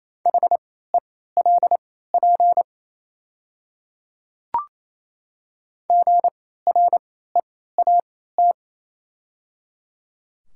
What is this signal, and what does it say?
Morse code, heard as a keyed tone, about 28 wpm, spaced as for 16 wpm, 700 Hz. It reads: HELP GREAT
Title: 3x Character Spacing - Top 200 Words - 28wpm